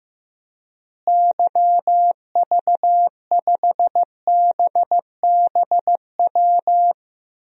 Morse code, heard as a keyed tone, about 16 wpm, 700 Hz. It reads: YV5BBW
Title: DXCC-100 Call Signs - 6 Characters - Repeated 3x - 15wpm